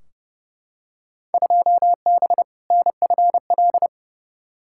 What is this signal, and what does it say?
Morse code, heard as a keyed tone, about 30 wpm, 700 Hz. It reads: 2B NFL
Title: Field Day Exchanges Encore - 30wpm